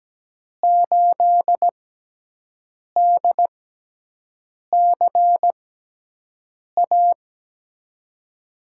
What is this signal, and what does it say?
Morse code, heard as a keyed tone, about 17 wpm, spaced as for 6 wpm, 700 Hz. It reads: 8DCA
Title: USA Call Signs - 6 Characters - 6x Character Spacing - 17wpm